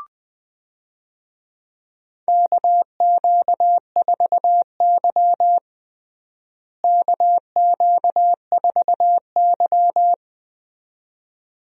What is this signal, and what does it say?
Morse code, heard as a keyed tone, about 19 wpm, 700 Hz. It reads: KQ4Y KQ4Y
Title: USA Call Signs - 4 Characters - Repeated 2x - 20wpm